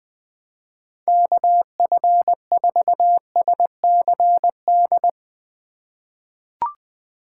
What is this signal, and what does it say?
Morse code, heard as a keyed tone, about 20 wpm, 700 Hz. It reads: KF4SCD
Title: USA Call Signs - 6 Characters - 2x Character Spacing - 20wpm